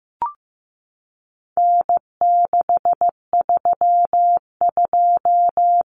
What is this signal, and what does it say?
Morse code, heard as a keyed tone, about 15 wpm, 700 Hz. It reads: N632
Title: Common Cut Numbers - 4-Digit 15wpm